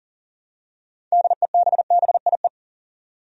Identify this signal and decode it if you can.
Morse code, heard as a keyed tone, about 37 wpm, 700 Hz. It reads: DEBBIE
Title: USA Common Names - 40wpm